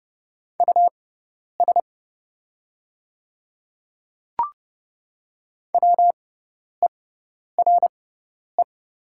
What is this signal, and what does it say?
Morse code, heard as a keyed tone, about 30 wpm, spaced as for 10 wpm, 700 Hz. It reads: US WERE
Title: 6x Character Spacing - Top 300 Words - 30wpm